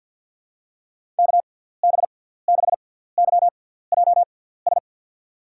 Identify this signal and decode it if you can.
Morse code, heard as a keyed tone, about 47 wpm, spaced as for 17 wpm, 700 Hz. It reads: KB6YJS
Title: USA Call Signs - 6x Character Spacing - 50wpm